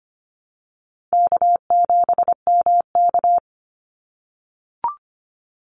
Morse code, heard as a keyed tone, about 25 wpm, 700 Hz. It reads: K7MK
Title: NA SOTA Activator Callsigns 25wpm